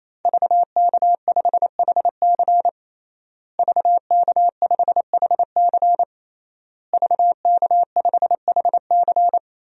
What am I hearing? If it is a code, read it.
Morse code, heard as a keyed tone, about 28 wpm, 700 Hz. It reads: VK5HC VK5HC VK5HC
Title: DXCC-100 Call Signs - 5 Characters - Repeated 3x - 28wpm